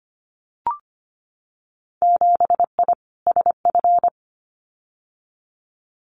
Morse code, heard as a keyed tone, about 25 wpm, 700 Hz. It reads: 7I SF
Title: Winter Field Day Exchanges Encore 25wpm